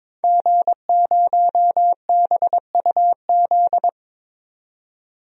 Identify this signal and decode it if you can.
Morse code, heard as a keyed tone, about 22 wpm, 700 Hz. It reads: G0BUZ
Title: DXCC-100 Call Signs - 5 Characters - 22wpm